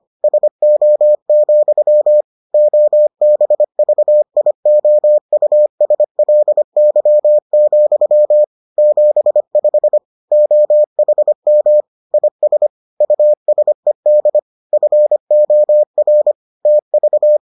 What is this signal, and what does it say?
Morse code, heard as a keyed tone, about 25 wpm, 600 Hz. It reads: SO, OBVIOUSLY, 75 OHM IS USED FOR TV